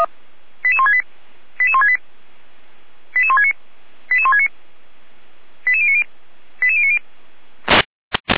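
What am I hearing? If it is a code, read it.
Five-tone selective call (ZVEI), heard as quick runs of five tones, tones of about 70 ms. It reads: 88168 88168 88169 88169 88090 88090
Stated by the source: Five-tone ZVEI selective call received on SDR.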